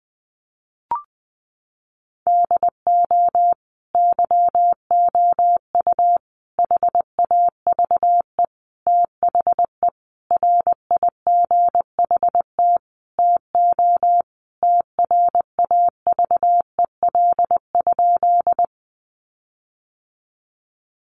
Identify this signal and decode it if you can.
Morse code, heard as a keyed tone, about 20 wpm, 700 Hz. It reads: DO YOU HAVE THE RIGHT TO TRAVEL?
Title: Sentences from Top 1400 Words 20wpm